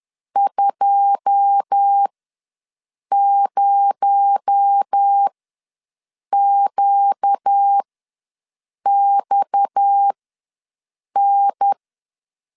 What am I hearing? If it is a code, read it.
Morse code, heard as a keyed tone, about 10 wpm, spaced as for 6 wpm, 800 Hz. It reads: 20QXN
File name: LEZIONI / 10 SIGNS / 10B INT ws 10wpm 4134